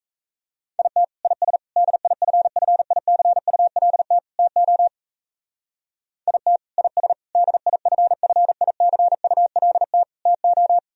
Morse code, heard as a keyed tone, about 32 wpm, 700 Hz. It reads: IT IS DIFFICULT TO IT IS DIFFICULT TO
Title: Speed-Racing - Sets of 4 Words 28wpm